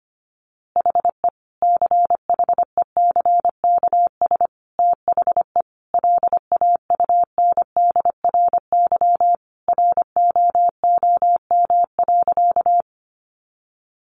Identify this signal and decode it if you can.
Morse code, heard as a keyed tone, about 25 wpm, 700 Hz. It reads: HE CHECKS THE LAUNDRY ROOM.
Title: Easy Phrases 25wpm